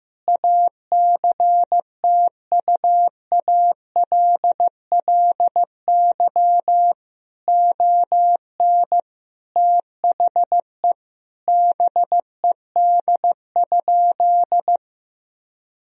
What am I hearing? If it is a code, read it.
Morse code, heard as a keyed tone, about 16 wpm, 700 Hz. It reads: ACTUALLY ON THE BED?
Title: Sentences from Top 1000 Words - Review Part 4 - 15wpm